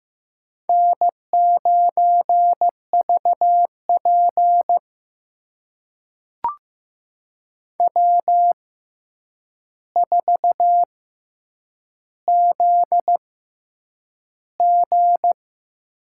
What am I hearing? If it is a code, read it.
Morse code, heard as a keyed tone, about 15 wpm, spaced as for 8 wpm, 700 Hz. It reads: N9VP W4ZG
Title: USA Call Signs - 4 Characters - 6x Character Spacing - 15wpm